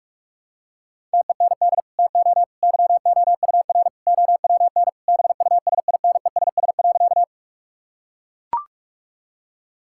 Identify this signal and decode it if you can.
Morse code, heard as a keyed tone, about 41 wpm, 700 Hz. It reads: TEND TO YOUR OWN BUSINESS.
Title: Sentences from Top 1000 Words - Review Part 5 - 45wpm